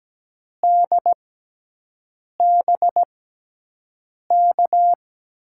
Morse code, heard as a keyed tone, about 17 wpm, spaced as for 6 wpm, 700 Hz. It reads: DBK
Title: USA Call Signs - 6x Character Spacing - 17wpm